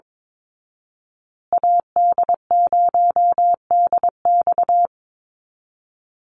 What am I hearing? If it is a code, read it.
Morse code, heard as a keyed tone, about 22 wpm, 700 Hz. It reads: AD0DX
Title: Call Signs Repeated 3x - 22wpm